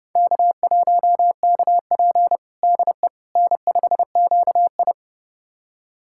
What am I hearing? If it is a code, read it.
Morse code, heard as a keyed tone, about 30 wpm, 700 Hz. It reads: K1KP DE N5QI
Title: USA Callsign DE Callsign - 4 Characters - 30wpm